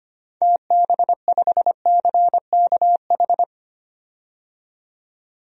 Morse code, heard as a keyed tone, about 25 wpm, 700 Hz. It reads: TB5CKH
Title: DXCC-100 Call Signs - 6 Characters - 25wpm